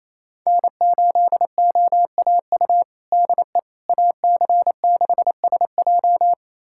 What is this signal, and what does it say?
Morse code, heard as a keyed tone, about 28 wpm, 700 Hz. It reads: N8OAU DE AC6SJ
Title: USA Callsign DE Callsign - 5 Characters - 28wpm